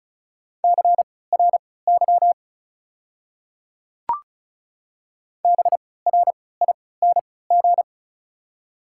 Morse code, heard as a keyed tone, about 34 wpm, spaced as for 20 wpm, 690 Hz. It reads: CRY BRING
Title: 3x Character Spacing - Top 500 Words - 35wpm